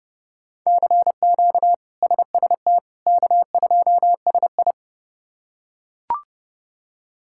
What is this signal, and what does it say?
Morse code, heard as a keyed tone, about 32 wpm, 700 Hz. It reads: CQ SST K2SI
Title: K1USN SST QSO Practice - 30wpm